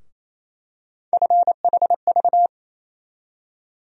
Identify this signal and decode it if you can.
Morse code, heard as a keyed tone, about 28 wpm, 700 Hz. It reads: FHV
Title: USA Call Sign Postfixes - 3 Letters 28wpm